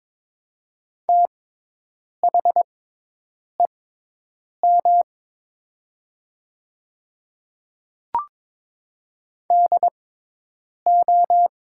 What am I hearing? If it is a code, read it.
Morse code, heard as a keyed tone, about 22 wpm, spaced as for 8 wpm, 700 Hz. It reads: THEM DO
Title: 6x Character Spacing - Top 100 Words - 22wpm